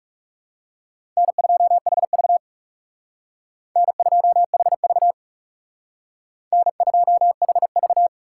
Morse code, heard as a keyed tone, about 39 wpm, 700 Hz. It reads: N2HV N2HV N2HV